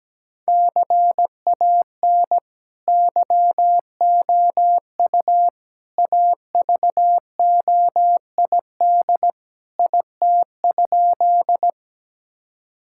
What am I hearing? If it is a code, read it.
Morse code, heard as a keyed tone, about 17 wpm, 700 Hz. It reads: CAN YOU AVOID IT?
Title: Sentences from Top 1000 Words - Review Part 6 - 17wpm